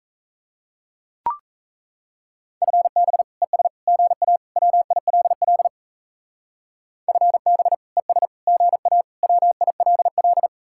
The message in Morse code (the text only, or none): FB ES GA WILL FB ES GA WILL